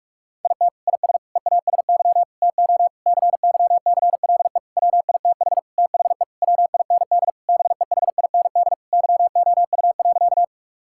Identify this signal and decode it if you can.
Morse code, heard as a keyed tone, about 46 wpm, 700 Hz. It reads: IT IS EASY TO CYCLE WITH THE WIND BEHIND YOU.
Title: Sentences from Top 1000 Words - Review Part 6 - 45wpm